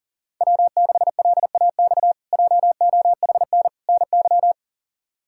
Morse code, heard as a keyed tone, about 40 wpm, 700 Hz. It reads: W6LAX JOHN NY
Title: K1USN SST QSO Practice - 40wpm